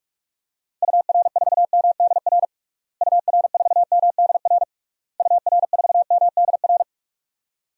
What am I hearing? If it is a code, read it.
Morse code, heard as a keyed tone, about 41 wpm, 700 Hz. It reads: UR4MDR UR4MDR UR4MDR
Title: DXCC-100 Call Signs - 6 Characters - Repeated 3x - 45wpm